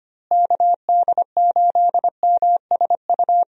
Morse code, heard as a keyed tone, about 25 wpm, 700 Hz. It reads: KD8MSU
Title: USA Call Signs - 6 Characters - Repeated 3x - 25wpm